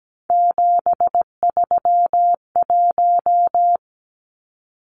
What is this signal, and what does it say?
Morse code, heard as a keyed tone, about 17 wpm, 700 Hz. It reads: 731
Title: Numbers - 3-Digit 17wpm